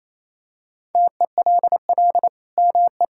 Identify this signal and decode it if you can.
Morse code, heard as a keyed tone, about 28 wpm, 700 Hz. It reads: TELL ME
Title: Sentences from Top 1000 Words - Review Part 5 - 28wpm